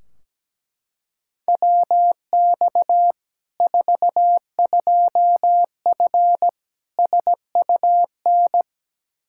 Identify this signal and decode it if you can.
Morse code, heard as a keyed tone, about 17 wpm, 700 Hz. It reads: WX 42F SUN
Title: K1USN SST QSO Practice 17wpm